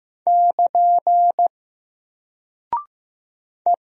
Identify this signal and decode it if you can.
Morse code, heard as a keyed tone, about 15 wpm, 700 Hz. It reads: <KN> E
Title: Single Character (Letter:Number:Punctuation:Prosign) Alt 15wpm